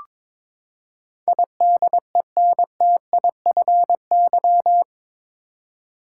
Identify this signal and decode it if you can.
Morse code, heard as a keyed tone, about 22 wpm, 700 Hz. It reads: IDENTIFY
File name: practice / 22wpm / Eight Letter Words - 22wpm